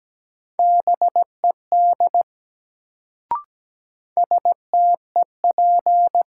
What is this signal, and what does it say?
Morse code, heard as a keyed tone, about 17 wpm, 700 Hz. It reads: BED STEP
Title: Sets of 5 Words Encore - New Word Drill - 17wpm